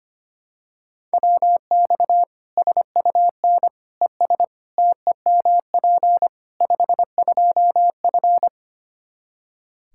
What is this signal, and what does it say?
Morse code, heard as a keyed tone, about 25 wpm, 700 Hz. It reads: WX SUN ES TEMP 52F
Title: Practice QSO - Exchange 2 - Info - 25wpm